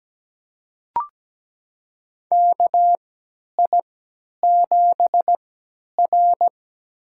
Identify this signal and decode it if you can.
Morse code, heard as a keyed tone, about 17 wpm, spaced as for 10 wpm, 700 Hz. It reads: KI7R
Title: USA Call Signs - 3x Character Spacing - 17wpm